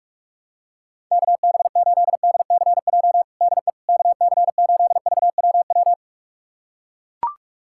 Morse code, heard as a keyed tone, about 42 wpm, 700 Hz. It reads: KB8DCJ DE KC8VWW
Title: USA Callsign DE Callsign - 6 Characters - 45wpm